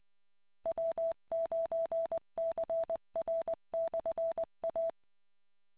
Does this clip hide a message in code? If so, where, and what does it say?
Yes: Morse code from 0.7 to 4.9 s, heard as a keyed tone, about 20 wpm, 660 Hz. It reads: W9CR/A